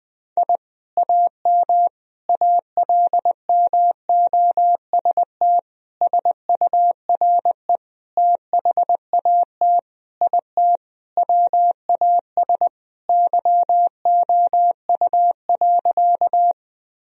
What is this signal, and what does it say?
Morse code, heard as a keyed tone, about 20 wpm, 700 Hz. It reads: I AM ALMOST SURE THAT IT WAS YOU.
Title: Sentences from Top 1000 Words - Review Part 4 - 20wpm